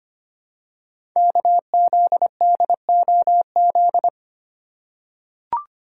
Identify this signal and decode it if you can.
Morse code, heard as a keyed tone, about 25 wpm, 700 Hz. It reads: KZDOZ
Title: Code Groups - 5 Letter:Numbers - 25wpm